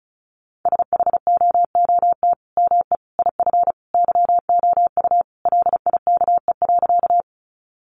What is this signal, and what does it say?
Morse code, heard as a keyed tone, about 34 wpm, 700 Hz. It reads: SHOOT ME IF YOU LIKE.
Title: Sentences from Top 1200 Words 35wpm